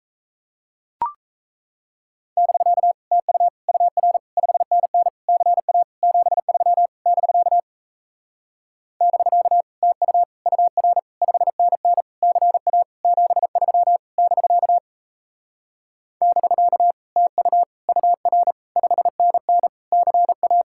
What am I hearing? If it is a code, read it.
Morse code, heard as a keyed tone, about 38 wpm, 700 Hz. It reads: <BK> TU UR 5NN CA 73 <BK> <BK> TU UR 5NN CA 73 <BK> <BK> TU UR 5NN CA